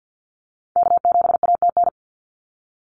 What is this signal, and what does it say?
Morse code, heard as a keyed tone, about 50 wpm, 700 Hz. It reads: K7ATN
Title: NA SOTA Activator Callsigns 50wpm